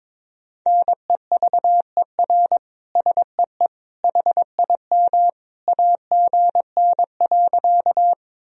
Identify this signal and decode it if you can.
Morse code, heard as a keyed tone, about 22 wpm, 700 Hz. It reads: NEVER SEE HIM AGN.
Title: Sentences from Top 300 Words and Abbreviations 22wpm